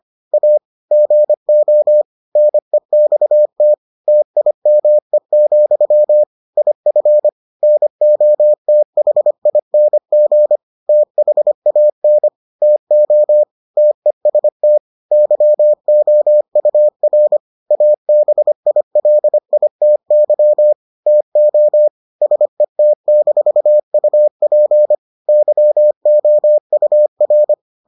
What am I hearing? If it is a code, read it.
Morse code, heard as a keyed tone, about 25 wpm, 600 Hz. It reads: A GO NEXT TIME, IF NOTHING THAN TO TEST YOUR ABILITY TO SET-UP YOUR